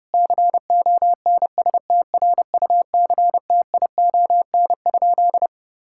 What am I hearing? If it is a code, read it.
Morse code, heard as a keyed tone, about 30 wpm, 700 Hz. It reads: CONSTRUCTION?
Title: Sentences from Top 1600 Words 30wpm